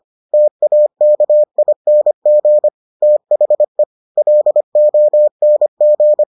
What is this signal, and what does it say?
Morse code, heard as a keyed tone, about 25 wpm, 600 Hz. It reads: TAKING THE LONG